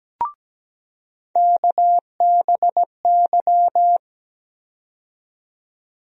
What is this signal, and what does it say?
Morse code, heard as a keyed tone, about 17 wpm, 700 Hz. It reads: KBY
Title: USA Call Sign Postfixes - 3 Letters 17wpm